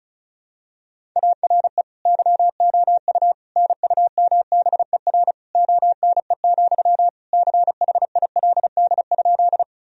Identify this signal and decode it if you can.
Morse code, heard as a keyed tone, about 36 wpm, 700 Hz. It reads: ARE YOU NUMBER ONE, CHILD?
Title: Sentences from Top 1000 Words - Review Part 1 - 35wpm